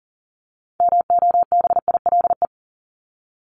Morse code, heard as a keyed tone, about 40 wpm, 700 Hz. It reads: MOBILE
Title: Top 1751-2000 Words 40wpm